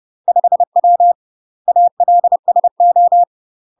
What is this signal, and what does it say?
Morse code, heard as a keyed tone, about 30 wpm, 700 Hz. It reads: 5W ALSO
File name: UK MASTER QSOs TRAINING / 30WPM / UK MASTER QSOs 0004